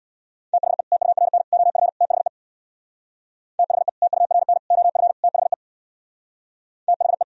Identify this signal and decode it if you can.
Morse code, heard as a keyed tone, about 64 wpm, 700 Hz. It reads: THE TURN OF THE THE TURN OF THE THE